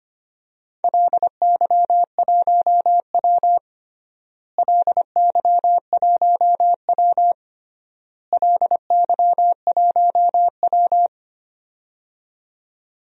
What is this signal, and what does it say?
Morse code, heard as a keyed tone, about 25 wpm, 700 Hz. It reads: LY1W LY1W LY1W
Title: DXCC-100 Call Signs - 4 Characters - Repeated 3x - 25wpm